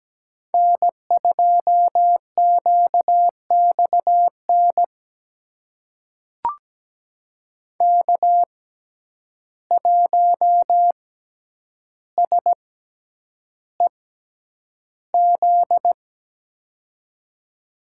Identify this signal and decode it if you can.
Morse code, heard as a keyed tone, about 17 wpm, spaced as for 9 wpm, 700 Hz. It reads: N2QXN K1SEZ
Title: USA Call Signs - 5 Characters - 6x Character Spacing - 17wpm